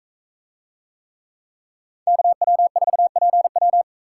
Morse code, heard as a keyed tone, about 43 wpm, 700 Hz. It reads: KW4PW